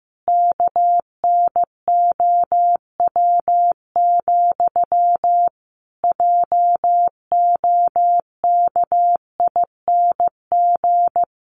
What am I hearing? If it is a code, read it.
Morse code, heard as a keyed tone, about 15 wpm, 700 Hz. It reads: KNOW, JOKING
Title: A Sound Of Thunder 15wpm